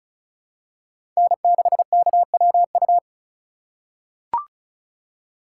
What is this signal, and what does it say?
Morse code, heard as a keyed tone, about 36 wpm, 700 Hz. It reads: N6KWU
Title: USA Call Signs - 5 Characters - 4x Character Spacing - 35wpm